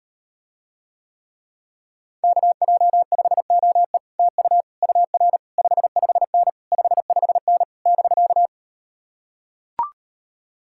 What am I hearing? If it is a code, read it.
Morse code, heard as a keyed tone, about 38 wpm, 700 Hz. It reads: KJ5OE TU UR 55N 55N <BK>